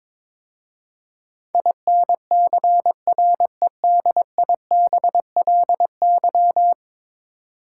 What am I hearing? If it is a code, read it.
Morse code, heard as a keyed tone, about 22 wpm, 700 Hz. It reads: INCREDIBLY
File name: practice / 22wpm / 10 Letter Words - First 3 Letters Given - 22wpm